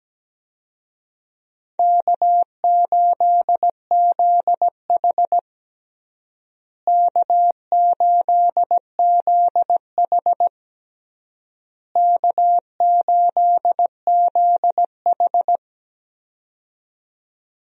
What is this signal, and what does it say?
Morse code, heard as a keyed tone, about 17 wpm, 700 Hz. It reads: K8ZH K8ZH K8ZH
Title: USA Call Signs - 4 Characters - Repeated 3x - 17wpm